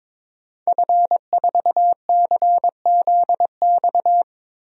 Morse code, heard as a keyed tone, about 22 wpm, 700 Hz. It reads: F4CZX
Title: DXCC-100 Call Signs - 5 Characters - 22wpm